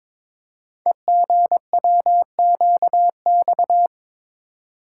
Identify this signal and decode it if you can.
Morse code, heard as a keyed tone, about 22 wpm, 700 Hz. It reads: EGWQX
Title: Code Groups - 5 Letters - 22wpm